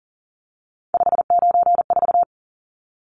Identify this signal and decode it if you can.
Morse code, heard as a keyed tone, about 39 wpm, 700 Hz. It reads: 594
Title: Numbers - 3-Digit 40wpm